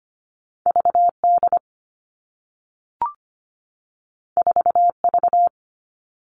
Morse code, heard as a keyed tone, about 25 wpm, 700 Hz. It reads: V D 4 V
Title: All Cut Numbers - 2-Digit 25wpm